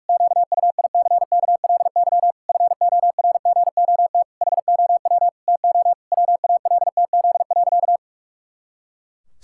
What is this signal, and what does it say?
Morse code, heard as a keyed tone, about 46 wpm, 700 Hz. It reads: QUICKLY FORGOT HOW TO WALTZ.
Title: Pangram Challenge v2 - 45wpm